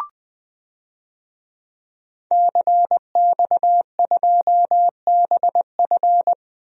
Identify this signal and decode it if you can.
Morse code, heard as a keyed tone, about 20 wpm, 700 Hz. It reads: CX2BF
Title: DXCC-100 Call Signs - 5 Characters - 20wpm